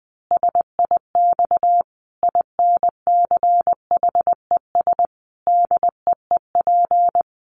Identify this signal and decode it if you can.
Morse code, heard as a keyed tone, about 20 wpm, 700 Hz. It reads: SIX INCHES DEEP
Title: A Sound Of Thunder 20wpm